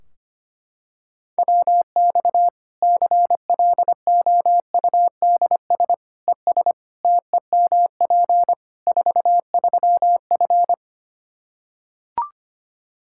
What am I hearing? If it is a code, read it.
Morse code, heard as a keyed tone, about 25 wpm, 700 Hz. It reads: WX CLOUDS ES TEMP 43F
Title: Practice QSO - Exchange 2 - Info - 25wpm